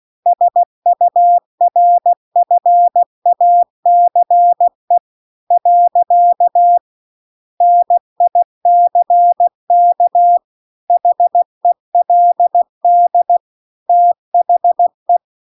Morse code, heard as a keyed tone, about 16 wpm, 700 Hz. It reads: SURFACE . NICK HELD THE